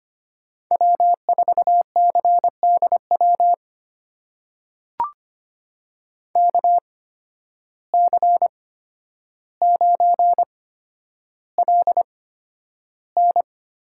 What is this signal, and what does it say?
Morse code, heard as a keyed tone, about 25 wpm, spaced as for 11 wpm, 700 Hz. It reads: W4CDW KC9LN
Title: USA Call Signs - 5 Characters - 8x Character Spacing - 25wpm